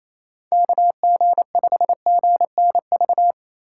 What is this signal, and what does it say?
Morse code, heard as a keyed tone, about 28 wpm, 700 Hz. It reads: KG5GNV
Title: USA Call Signs - 6 Characters - 4x Character Spacing - 28wpm